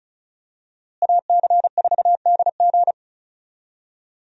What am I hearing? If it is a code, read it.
Morse code, heard as a keyed tone, about 36 wpm, 700 Hz. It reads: AC4DG
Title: USA Call Signs - 5 Characters - Repeated 3x - 35wpm